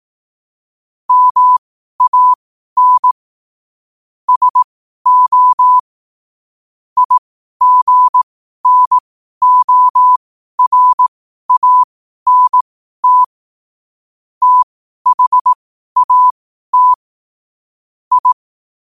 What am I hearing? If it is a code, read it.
Morse code, heard as a keyed tone, about 18 wpm, spaced as for 12 wpm, 1000 Hz. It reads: MAN SO IGNORANT THAT I
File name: qotdfiles / 12wpm